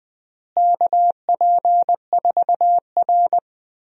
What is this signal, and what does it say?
Morse code, heard as a keyed tone, about 20 wpm, 700 Hz. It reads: KP4R
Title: DXCC-100 Call Signs - 4 Characters - Repeated 3x - 20wpm